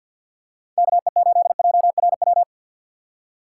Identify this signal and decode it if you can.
Morse code, heard as a keyed tone, about 51 wpm, 700 Hz. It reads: KE8JRW
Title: USA Call Signs - 3x Character Spacing - 50wpm